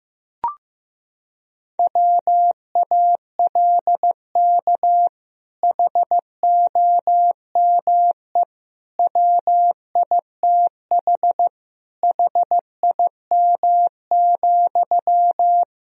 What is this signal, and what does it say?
Morse code, heard as a keyed tone, about 15 wpm, 700 Hz. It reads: WALK HOME WITH HIM,
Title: Sentences from Top 1000 Words - Review Part 1 - 15wpm